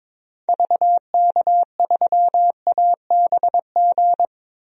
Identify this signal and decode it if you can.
Morse code, heard as a keyed tone, about 22 wpm, 700 Hz. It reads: VK3ABG
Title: DXCC-100 Call Signs - 6 Characters - 22wpm